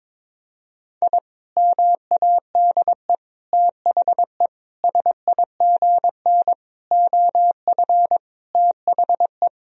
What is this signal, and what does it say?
Morse code, heard as a keyed tone, about 22 wpm, 700 Hz. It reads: I MADE THE SIGN OF THE
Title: Sentences from Top 1000 Words - Review Part 1 - 22wpm